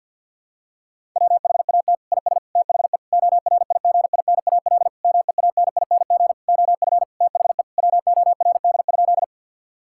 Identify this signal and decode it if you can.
Morse code, heard as a keyed tone, about 50 wpm, 700 Hz. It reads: WHAT IS THE ORIGINAL MEANING OF THE WORD?
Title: Sentences from Top 1000 Words - Review Part 6 - 50wpm